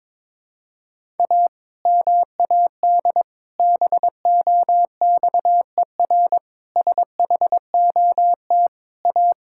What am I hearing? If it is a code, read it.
Morse code, heard as a keyed tone, about 22 wpm, 700 Hz. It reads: A MAD BOXER SHOT A
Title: Pangram Challenge v2 - 22wpm